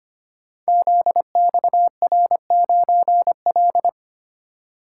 Morse code, heard as a keyed tone, about 25 wpm, 700 Hz. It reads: ZXR9L
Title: Code Groups - 5 Letter:Numbers - 25wpm